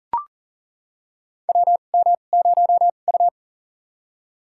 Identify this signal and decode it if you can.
Morse code, heard as a keyed tone, about 40 wpm, spaced as for 29 wpm, 700 Hz. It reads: WM0U
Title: USA Call Signs - 2x Character Spacing - 40wpm